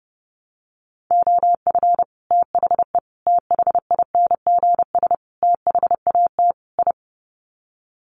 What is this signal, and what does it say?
Morse code, heard as a keyed tone, about 30 wpm, 700 Hz. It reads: OF THE THINGS THAT I
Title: Sets of 5 Words Encore 30wpm